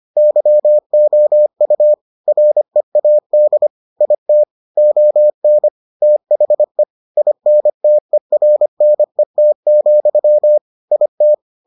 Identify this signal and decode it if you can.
Morse code, heard as a keyed tone, about 25 wpm, 600 Hz. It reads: YOU READ IT ON THE INTERNET, IT